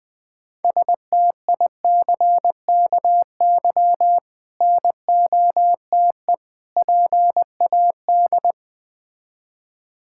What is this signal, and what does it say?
Morse code, heard as a keyed tone, about 20 wpm, 700 Hz. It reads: STICKY NOTE PAD
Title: Difficult Words with Hints - 20wpm